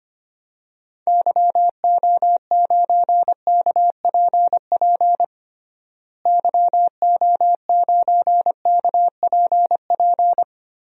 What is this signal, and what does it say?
Morse code, heard as a keyed tone, about 25 wpm, 700 Hz. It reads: YO9KPP YO9KPP
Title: DXCC-100 Call Signs - 6 Characters - Repeated 2x - 25wpm